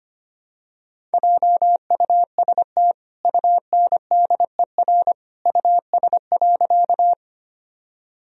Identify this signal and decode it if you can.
Morse code, heard as a keyed tone, about 25 wpm, 700 Hz. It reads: JUST UNDER US.
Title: Sentences from Top 1000 Words - Review Part 1 - 25wpm